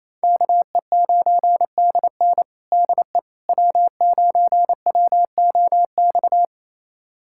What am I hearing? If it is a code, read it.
Morse code, heard as a keyed tone, about 28 wpm, 700 Hz. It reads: KE9DN DE W9WOX
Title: USA Callsign DE Callsign - 5 Characters - 28wpm